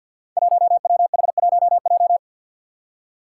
Morse code, heard as a keyed tone, about 47 wpm, 700 Hz. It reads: 1WH1J
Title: Five Challenging Letter-Number Combinations 50wpm